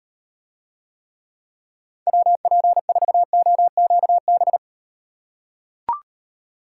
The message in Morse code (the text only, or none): WP4OQB